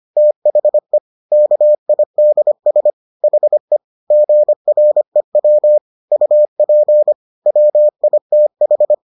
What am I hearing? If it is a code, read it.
Morse code, heard as a keyed tone, about 25 wpm, 600 Hz. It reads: THE KIDS HE GREW UP WITH